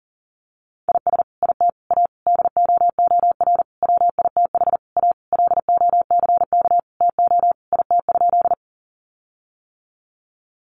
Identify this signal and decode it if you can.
Morse code, heard as a keyed tone, about 40 wpm, 700 Hz. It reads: IS IT A DOOR WITH A LOCK TO IT?
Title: Sentences from Top 1300 Words 40wpm